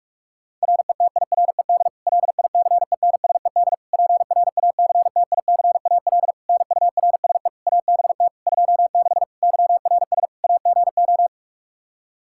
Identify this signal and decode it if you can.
Morse code, heard as a keyed tone, about 47 wpm, 700 Hz. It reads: RETIRED LICENSED PRACTICAL NURSE ABT 26 YRS AGO
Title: Practice QSO - Exchange 3 - More Info - 45wpm